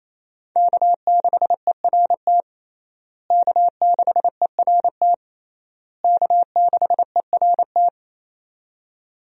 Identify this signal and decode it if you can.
Morse code, heard as a keyed tone, about 27 wpm, 700 Hz. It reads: K6ERT K6ERT K6ERT
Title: USA Call Signs - 5 Characters - Repeated 3x - 28wpm